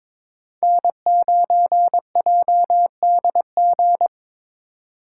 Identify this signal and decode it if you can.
Morse code, heard as a keyed tone, about 22 wpm, 700 Hz. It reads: N9JDG
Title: USA Call Signs - 3x Character Spacing - 22wpm